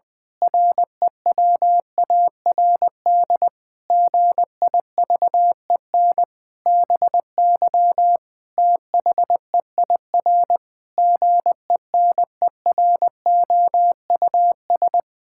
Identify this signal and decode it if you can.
Morse code, heard as a keyed tone, about 20 wpm, 700 Hz. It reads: REWARD GIVEN BY THEIR GENEROUS
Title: Pangram Challenge v2 - 20wpm